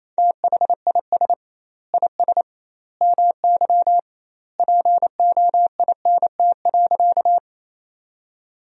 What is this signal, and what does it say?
Morse code, heard as a keyed tone, about 28 wpm, 700 Hz. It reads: THIS IS MY POINT.
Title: 2x Word-Spacing - Sentences from Top 200 Words - 28wpm